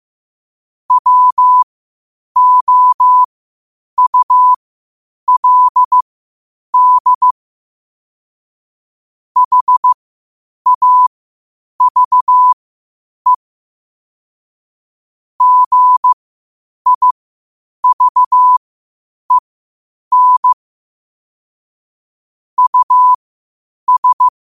Morse code, heard as a keyed tone, about 15 wpm, spaced as for 8 wpm, 1000 Hz. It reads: WOULD HAVE GIVEN US
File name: qotdfiles / 8wpm